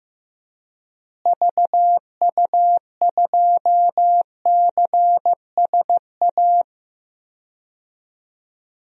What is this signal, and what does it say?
Morse code, heard as a keyed tone, about 15 wpm, 700 Hz. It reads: VU2CSA
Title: DXCC-100 Call Signs - 6 Characters - Repeated 2x - 15wpm